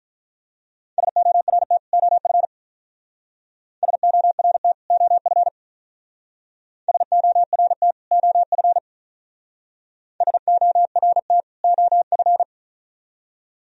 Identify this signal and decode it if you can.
Morse code, heard as a keyed tone, about 41 wpm, 700 Hz. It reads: SORT OF SORT OF SORT OF SORT OF
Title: Speed-Racing - Sets of 2 Words 35wpm